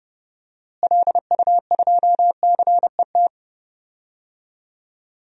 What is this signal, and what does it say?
Morse code, heard as a keyed tone, about 29 wpm, 700 Hz. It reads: LU2CET
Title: DXCC-100 Call Signs - 6 Characters - Repeated 2x - 30wpm